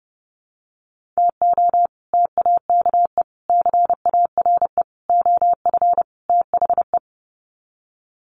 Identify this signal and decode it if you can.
Morse code, heard as a keyed tone, about 28 wpm, 700 Hz. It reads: TO TAKE CARE OF THE
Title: Sets of 5 Words Encore 30wpm